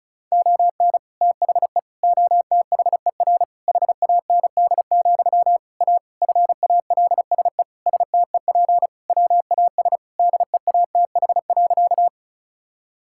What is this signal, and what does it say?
Morse code, heard as a keyed tone, about 36 wpm, 700 Hz. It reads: ON THE OTHER HAND, A FALSE STEP WAS DEATH.
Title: Sentences from Top 1600 Words 35wpm